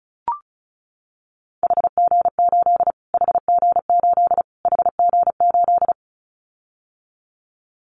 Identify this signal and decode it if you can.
Morse code, heard as a keyed tone, about 34 wpm, 700 Hz. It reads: HG8 HG8 HG8
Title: Callsign Prefixes for Top 25 DXCC - Repeated 3x - 35wpm